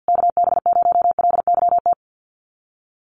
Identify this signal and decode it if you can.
Morse code, heard as a keyed tone, about 50 wpm, 700 Hz. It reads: KB0LYT
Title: USA Call Signs Repeated 3x 50wpm